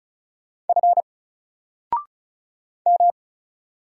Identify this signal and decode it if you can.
Morse code, heard as a keyed tone, about 36 wpm, 700 Hz. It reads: F M
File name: Single-Letter-Number-35wpm